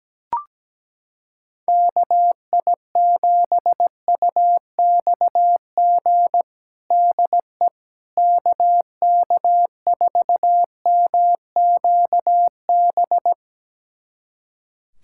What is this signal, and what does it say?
Morse code, heard as a keyed tone, about 17 wpm, 700 Hz. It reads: KI7UXG DE KK4MQB
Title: USA Callsign DE Callsign - 6 Characters - 17wpm